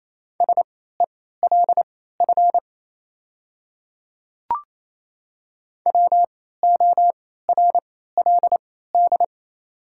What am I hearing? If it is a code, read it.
Morse code, heard as a keyed tone, about 28 wpm, spaced as for 16 wpm, 690 Hz. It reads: SELF WORLD
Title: 3x Character Spacing - Top 200 Words - 28wpm